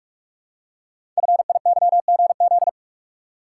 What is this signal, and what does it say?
Morse code, heard as a keyed tone, about 44 wpm, 700 Hz. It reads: FIYGZ
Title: Code Groups - 5 Letters - 45wpm